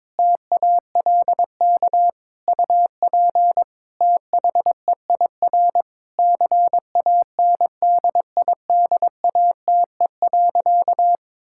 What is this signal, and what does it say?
Morse code, heard as a keyed tone, about 22 wpm, 700 Hz. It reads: TALK UP THEIR CANDIDATE.
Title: Sentences from Top 1600 Words 22wpm